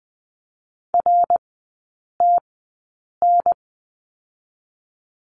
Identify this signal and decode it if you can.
Morse code, heard as a keyed tone, about 20 wpm, spaced as for 8 wpm, 700 Hz. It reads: RTN
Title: 114 Lesson 19e - Letter w - Three Characters 20wpm